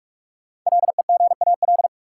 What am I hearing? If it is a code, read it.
Morse code, heard as a keyed tone, about 44 wpm, 700 Hz. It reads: LEGAL
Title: Sentences from Top 1200 Words - New Word Drill Alt - 45wpm